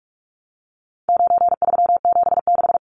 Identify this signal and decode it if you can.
Morse code, heard as a keyed tone, about 44 wpm, 700 Hz. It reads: 9376
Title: Numbers - 4-Digit 45wpm